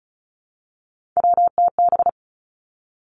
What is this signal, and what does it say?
Morse code, heard as a keyed tone, about 36 wpm, 700 Hz. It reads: WTB
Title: Three Challenging Letter-Number Combinations 35wpm